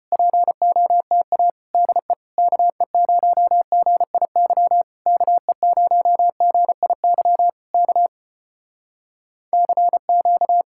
Morse code, heard as a keyed tone, about 33 wpm, 700 Hz. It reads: POTA DE KE0GIY KE0GIY K CQ